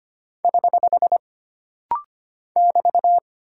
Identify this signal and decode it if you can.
Morse code, heard as a keyed tone, about 25 wpm, 700 Hz. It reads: <HH> <BT>
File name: Single Punctuation-Prosign - Rapid-Fire - 25wpm